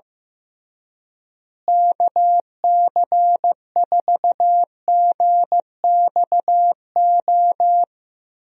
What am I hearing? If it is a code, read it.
Morse code, heard as a keyed tone, about 16 wpm, 700 Hz. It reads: KC4GXO
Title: USA Call Signs - 6 Characters - Repeated 3x - 15wpm